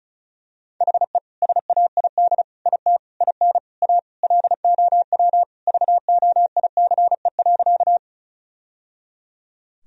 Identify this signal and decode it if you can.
Morse code, heard as a keyed tone, about 36 wpm, 700 Hz. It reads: HE SAID IT IN A LOW VOICE.
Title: Sentences from Top 1000 Words - Review Part 1 - 35wpm